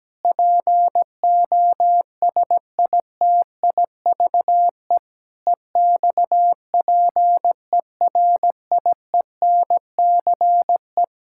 Morse code, heard as a keyed tone, about 17 wpm, 700 Hz. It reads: POSITIVE EXPERIENCE
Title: Sentences from Top 1000 Words - Review Part 5 - 17wpm